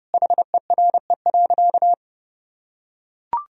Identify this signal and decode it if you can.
Morse code, heard as a keyed tone, about 32 wpm, 700 Hz. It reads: HERE.
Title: Sentences from Top 1000 Words - Review Part 2 - 30wpm